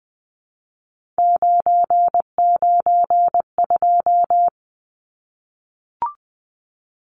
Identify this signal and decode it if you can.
Morse code, heard as a keyed tone, about 20 wpm, 700 Hz. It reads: 992
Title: Numbers - 3-Digit 20wpm